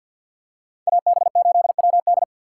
Morse code, heard as a keyed tone, about 47 wpm, 700 Hz. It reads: AB8WD
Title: USA Call Signs - 5 Characters - 4x Character Spacing - 50wpm